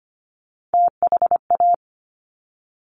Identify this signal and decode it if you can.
Morse code, heard as a keyed tone, about 25 wpm, 700 Hz. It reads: THA
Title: Top Three Letter Combinations in Words 25wpm